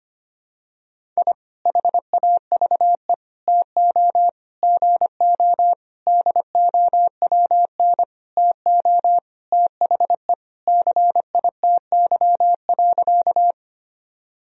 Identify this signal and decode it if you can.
Morse code, heard as a keyed tone, about 25 wpm, 700 Hz. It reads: I HAVE TO GO DOWN TO THE CITY.
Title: Sentences from Top 1000 Words - Review Part 1 - 25wpm